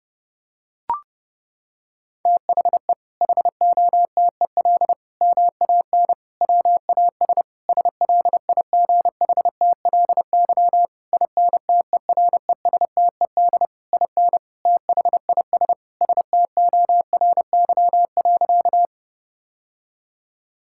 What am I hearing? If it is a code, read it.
Morse code, heard as a keyed tone, about 32 wpm, 700 Hz. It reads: THE HOTEL MAN WAS SLIGHTLY INTERESTED IN THIS STORY.
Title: Sentences from Top 1600 Words 30wpm